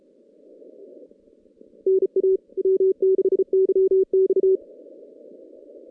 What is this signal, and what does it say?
Morse code, heard as a keyed tone, about 34 wpm, 440 Hz. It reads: NA W6YX
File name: 2016 JAN CW NAQP / 2016 Jan naqp k5ot - 0-8